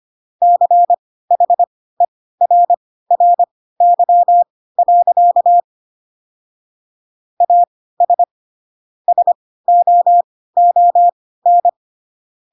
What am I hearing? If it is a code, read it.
Morse code, heard as a keyed tone, about 25 wpm, spaced as for 16 wpm, 700 Hz. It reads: CHERRY. AS SOON
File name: Pinocchio / Pinocchio ch1 10 25 ik1hge